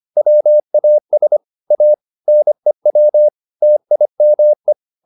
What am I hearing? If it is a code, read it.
Morse code, heard as a keyed tone, about 25 wpm, 600 Hz. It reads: WAS A NEW TIME
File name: the old man and the sea part 1 25wpm